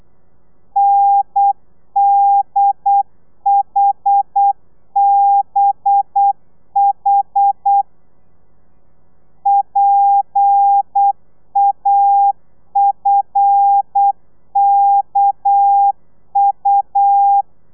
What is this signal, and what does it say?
Morse code, heard as a keyed tone, about 8 wpm, 800 Hz. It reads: NDHBH PAFKU
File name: telegrafia / i2tfj / Lez11 35